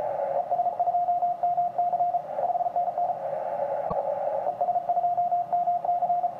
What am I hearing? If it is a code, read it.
Morse code, heard as a keyed tone, about 33 wpm, 710 Hz, with radio noise. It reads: ES2MC 5NN ES2MC
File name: ti9a 80cw